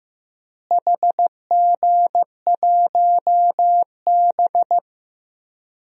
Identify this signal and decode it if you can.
Morse code, heard as a keyed tone, about 15 wpm, 700 Hz. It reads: HG1B
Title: DXCC-100 Call Signs - 4 Characters - Repeated 3x - 15wpm